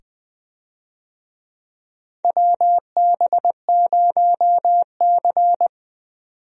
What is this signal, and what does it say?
Morse code, heard as a keyed tone, about 20 wpm, 700 Hz. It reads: WB0C
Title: USA Call Signs - 4 Characters - 20wpm